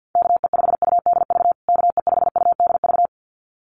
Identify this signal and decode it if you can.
Morse code, heard as a keyed tone, about 50 wpm, 700 Hz. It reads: KE5UDV KE5UDV
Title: USA Call Signs Repeated 2x 50wpm